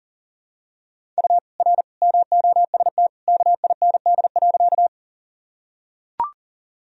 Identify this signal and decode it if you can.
Morse code, heard as a keyed tone, about 40 wpm, 700 Hz. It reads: U R MOST KIND.
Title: Sentences from Top 200 Words and Abbreviations 40wpm